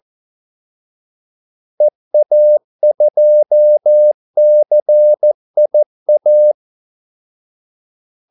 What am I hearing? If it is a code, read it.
Morse code, heard as a keyed tone, about 14 wpm, 600 Hz. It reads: EA2CIA
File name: Callsigns-14wpm